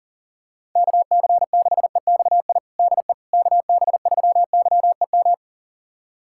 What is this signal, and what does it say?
Morse code, heard as a keyed tone, about 40 wpm, 700 Hz. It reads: KC6EXI DE KB3YEM
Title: USA Callsign DE Callsign 40wpm